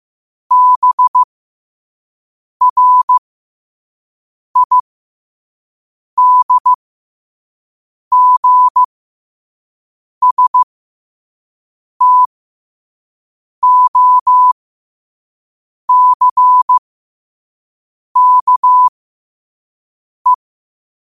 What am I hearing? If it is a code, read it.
Morse code, heard as a keyed tone, about 15 wpm, spaced as for 5 wpm, 1000 Hz. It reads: BRIDGSTOCKE
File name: qotdfiles / 5wpm